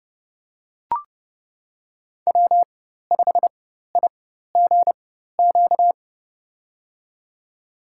Morse code, heard as a keyed tone, about 31 wpm, spaced as for 14 wpm, 700 Hz. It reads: W5IGQ
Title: USA Call Signs - 5 Characters - 4x Character Spacing - 30wpm